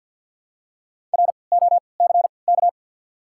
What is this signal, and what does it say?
Morse code, heard as a keyed tone, about 54 wpm, spaced as for 29 wpm, 690 Hz. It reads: ROCK
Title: 3x Character Spacing - Top 500 Words - 50wpm